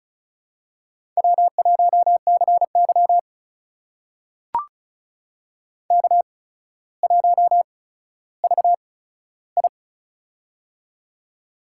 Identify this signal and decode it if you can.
Morse code, heard as a keyed tone, about 36 wpm, spaced as for 15 wpm, 700 Hz. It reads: W1CY K1VI
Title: USA Call Signs - 4 Characters - 8x Character Spacing - 35wpm